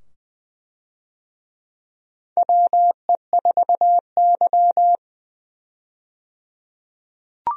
Morse code, heard as a keyed tone, about 20 wpm, 700 Hz. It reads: WE4Y
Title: USA Call Signs - 4 Characters - 20wpm